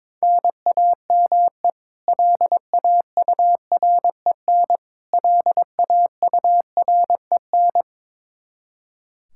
Morse code, heard as a keyed tone, about 22 wpm, 700 Hz. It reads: NAME LAUREN LAUREN
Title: Practice QSO - Exchange 1 - Beginning - 22wpm